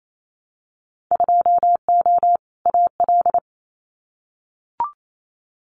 Morse code, heard as a keyed tone, about 28 wpm, 700 Hz. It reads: 2O AL
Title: Winter Field Day Exchanges 28wpm